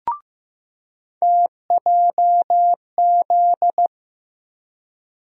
Morse code, heard as a keyed tone, about 15 wpm, 700 Hz. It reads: TJZ
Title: USA Call Sign Postfixes - 3 Letters 15wpm